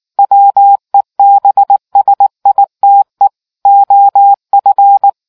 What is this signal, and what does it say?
Morse code, heard as a keyed tone, about 19 wpm, 800 Hz. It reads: WEBSITE OF